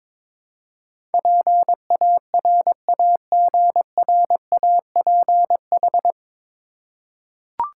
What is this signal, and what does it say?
Morse code, heard as a keyed tone, about 22 wpm, 700 Hz. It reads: PARAGRAPH
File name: 9 Letter Words - First 3 Letters Given - 22wpm